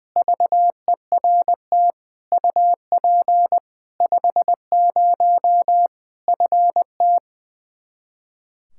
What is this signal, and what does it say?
Morse code, heard as a keyed tone, about 20 wpm, 700 Hz. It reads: VERT UP 50 FT
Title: Practice QSO - Exchange 2 - Info - 20wpm